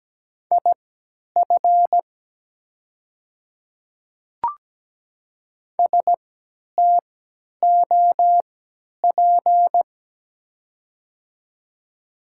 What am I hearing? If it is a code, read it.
Morse code, heard as a keyed tone, about 17 wpm, spaced as for 10 wpm, 700 Hz. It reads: IF STOP
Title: 3x Character Spacing - Top 500 Words - 17wpm